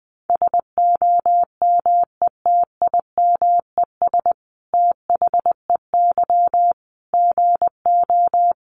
Morse code, heard as a keyed tone, about 20 wpm, 700 Hz. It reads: SOMETIMES THEY GO
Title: Sentences from Top 600 Words 20wpm